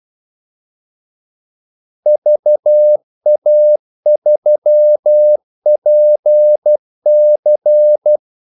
Morse code, heard as a keyed tone, about 12 wpm, 600 Hz. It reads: VA3PC
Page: 2 ekstra ordmellemrum mellem kaldesignalerne